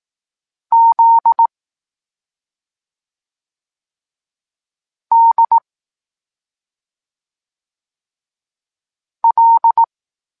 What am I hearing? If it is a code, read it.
Morse code, heard as a keyed tone, about 18 wpm, 920 Hz. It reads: Z D L
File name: Tekst-23 4